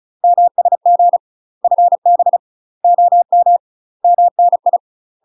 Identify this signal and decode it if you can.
Morse code, heard as a keyed tone, about 36 wpm, 700 Hz. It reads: MSG FB OM MNI
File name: UK MASTER QSOs TRAINING / 35WPM / UK MASTER QSOs 0013